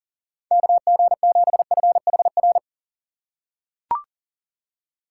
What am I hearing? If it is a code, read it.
Morse code, heard as a keyed tone, about 40 wpm, 700 Hz. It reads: KG7FHR
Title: USA Call Signs - 6 Characters - 2x Character Spacing - 40wpm